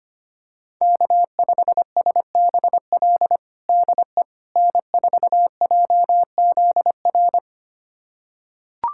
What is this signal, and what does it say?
Morse code, heard as a keyed tone, about 25 wpm, 700 Hz. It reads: K5SBL DE N4JZR
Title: USA Callsign DE Callsign - 5 Characters - 25wpm